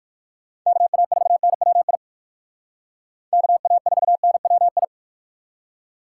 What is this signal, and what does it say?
Morse code, heard as a keyed tone, about 60 wpm, 700 Hz. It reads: KA4NWI KA4NWI